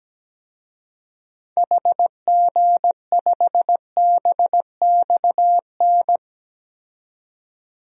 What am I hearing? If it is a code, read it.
Morse code, heard as a keyed tone, about 17 wpm, 700 Hz. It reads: HG5BXN
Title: DXCC-100 Call Signs - 6 Characters - 17wpm